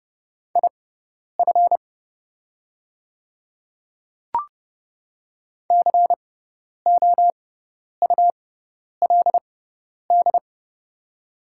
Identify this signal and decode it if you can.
Morse code, heard as a keyed tone, about 30 wpm, spaced as for 10 wpm, 700 Hz. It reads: IF COULD
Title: 6x Character Spacing - Top 100 Words - 30wpm